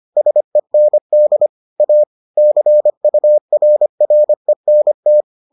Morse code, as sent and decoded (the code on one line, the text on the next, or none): ... -. -.. / .- / -.-. ..- .-. .-. -. -
SEND A CURRENT